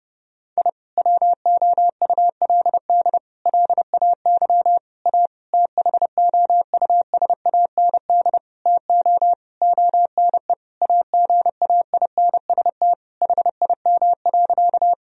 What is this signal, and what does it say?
Morse code, heard as a keyed tone, about 31 wpm, 700 Hz. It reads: I WOULD LAY A THOUSAND TO ONE AGAINST HIM.
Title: Sentences from Top 1000 Words - Review Part 3 - 30wpm